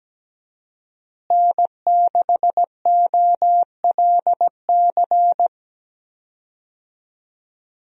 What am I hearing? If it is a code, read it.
Morse code, heard as a keyed tone, about 17 wpm, 700 Hz. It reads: N6OLC
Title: USA Call Signs - 5 Characters - Repeated 2x - 17wpm